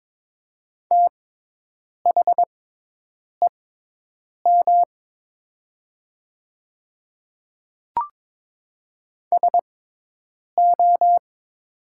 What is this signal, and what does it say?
Morse code, heard as a keyed tone, about 22 wpm, spaced as for 8 wpm, 700 Hz. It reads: THEM SO
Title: 6x Character Spacing - Top 100 Words - 22wpm